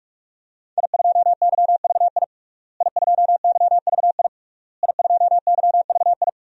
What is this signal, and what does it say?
Morse code, heard as a keyed tone, about 41 wpm, 700 Hz. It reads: I2YVI I2YVI I2YVI
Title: DXCC-100 Call Signs - 5 Characters - Repeated 3x - 45wpm